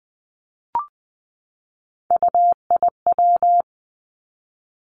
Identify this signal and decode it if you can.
Morse code, heard as a keyed tone, about 20 wpm, 700 Hz. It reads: UIW
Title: Three Challenging Letter-Number Combinations 20wpm